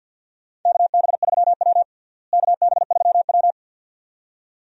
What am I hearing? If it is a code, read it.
Morse code, heard as a keyed tone, about 52 wpm, 700 Hz. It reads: KB3W KB3W
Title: USA Call Signs - 4 Characters - Repeated 2x - 50wpm